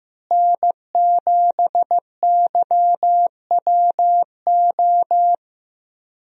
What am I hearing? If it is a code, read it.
Morse code, heard as a keyed tone, about 15 wpm, 700 Hz. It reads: N7YWO
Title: USA Call Signs - 5 Characters - Repeated 3x - 15wpm